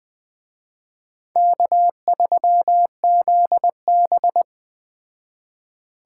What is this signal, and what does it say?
Morse code, heard as a keyed tone, about 20 wpm, 700 Hz. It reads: K3ZB
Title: Speed Racing - USA 1x2 Call Signs 17wpm